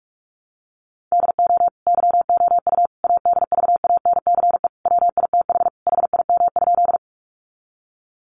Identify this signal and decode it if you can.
Morse code, heard as a keyed tone, about 44 wpm, 700 Hz. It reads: DO YOU ADVANCE WITH HIM?
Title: Sentences from Top 1400 Words 45wpm